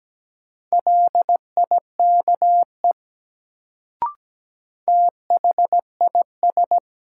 Morse code, heard as a keyed tone, about 17 wpm, 700 Hz. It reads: LIKE THIS
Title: Sentences from Top 100 Words - New Word Drill - 17wpm